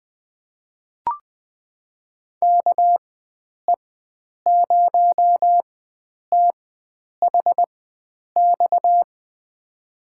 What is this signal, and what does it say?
Morse code, heard as a keyed tone, about 21 wpm, spaced as for 9 wpm, 700 Hz. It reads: KE0THX
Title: USA Call Signs - 6 Characters - 4x Character Spacing - 20wpm